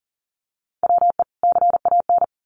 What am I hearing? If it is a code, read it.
Morse code, heard as a keyed tone, about 40 wpm, 700 Hz. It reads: WE CAN
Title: Sets of 2 Words 40wpm